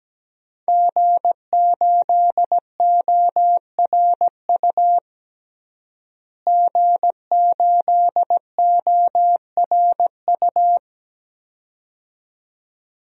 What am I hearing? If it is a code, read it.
Morse code, heard as a keyed tone, about 17 wpm, 700 Hz. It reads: G8ORU G8ORU
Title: DXCC-100 Call Signs - 5 Characters - Repeated 2x - 17wpm